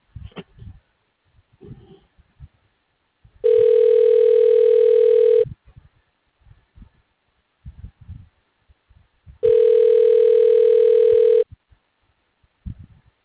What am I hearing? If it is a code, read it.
Ringback tone (North America), 440 + 480 Hz.